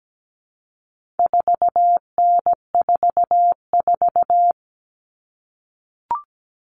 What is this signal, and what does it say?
Morse code, heard as a keyed tone, about 17 wpm, 700 Hz. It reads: 4N44
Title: All Cut Numbers - 4-Digit 17wpm